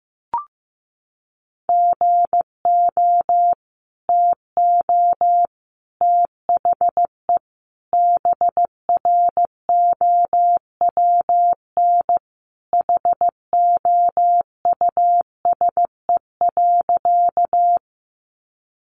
Morse code, heard as a keyed tone, about 14 wpm, 700 Hz. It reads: GO TO THE BROWN HOUSE.
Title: Sentences from Top 1200 Words 15wpm